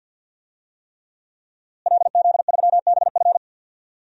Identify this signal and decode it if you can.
Morse code, heard as a keyed tone, about 50 wpm, 700 Hz. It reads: LZ3BF
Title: DXCC-100 Call Signs - 5 Characters - Repeated 3x - 50wpm